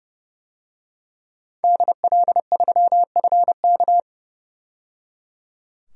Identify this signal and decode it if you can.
Morse code, heard as a keyed tone, about 30 wpm, 700 Hz. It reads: DL3FK
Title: DXCC-100 Call Signs - 5 Characters - 30wpm